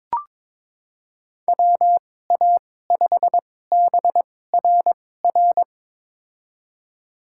Morse code, heard as a keyed tone, about 22 wpm, spaced as for 16 wpm, 700 Hz. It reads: WA5BRR
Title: USA Call Signs - 6 Characters - 2x Character Spacing - 22wpm